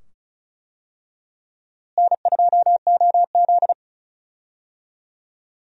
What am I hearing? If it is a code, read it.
Morse code, heard as a keyed tone, about 36 wpm, 700 Hz. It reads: N2OZ
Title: USA Call Signs - 4 Characters - Repeated 2x - 35wpm